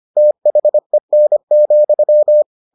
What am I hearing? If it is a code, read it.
Morse code, heard as a keyed tone, about 25 wpm, 600 Hz. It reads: THEN,